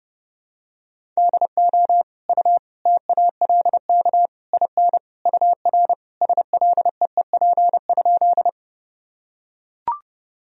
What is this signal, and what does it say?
Morse code, heard as a keyed tone, about 30 wpm, 700 Hz. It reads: DO U TALK IN UR SLEEP?
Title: Sentences from Top 500 Words and Abbreviations 30wpm